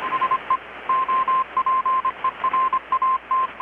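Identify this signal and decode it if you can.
Morse code, heard as a keyed tone, about 25 wpm, 1050 Hz, with radio noise. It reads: SE OPERAT
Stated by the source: Morse from KPH's Night of Nights XX maritime radiotelegraphy event, received in Chiapas, Mexico about 3,800 km away on a Kenwood R-600 with a grounded 10 m long-wire antenna.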